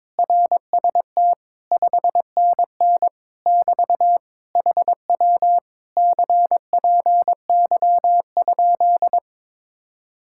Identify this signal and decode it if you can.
Morse code, heard as a keyed tone, about 22 wpm, 700 Hz. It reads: RST 5NN <BT> HW CPY?
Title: Speed-Racing Common QSO Phrases - 22wpm